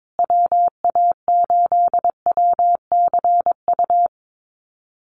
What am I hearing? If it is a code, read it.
Morse code, heard as a keyed tone, about 22 wpm, 700 Hz. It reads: WA8WCU
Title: USA Call Signs Repeated 3x 22wpm